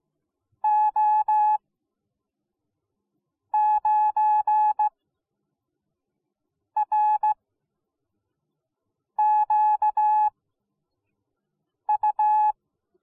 Morse code, heard as a keyed tone, about 17 wpm, spaced as for 4 wpm, 850 Hz. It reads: O9RQU